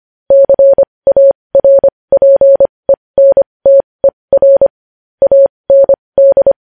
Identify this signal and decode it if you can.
Morse code, heard as a keyed tone, about 25 wpm, spaced as for 20 wpm, 550 Hz. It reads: CARPENTER AND
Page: news-headlines-in-morse-code-at-20-wpm